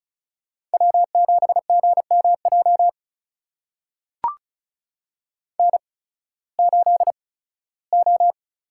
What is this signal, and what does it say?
Morse code, heard as a keyed tone, about 34 wpm, spaced as for 18 wpm, 700 Hz. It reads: W7GMJ N8O
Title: USA Call Signs - 5 Characters - 8x Character Spacing - 35wpm